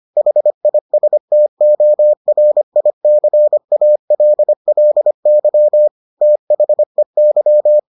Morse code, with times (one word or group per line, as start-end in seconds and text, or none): 0.2-5.9 HISTORICALLY
6.2-7.8 THEY